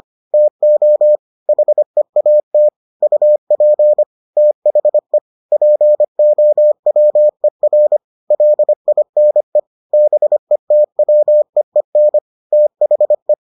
Morse code, heard as a keyed tone, about 25 wpm, 600 Hz. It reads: TO HEAT UP THE POWER LINE BETWEEN THE